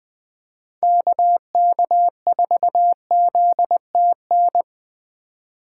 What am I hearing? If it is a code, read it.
Morse code, heard as a keyed tone, about 20 wpm, 700 Hz. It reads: KK4ZTN
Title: USA Call Signs - 4x Character Spacing - 20wpm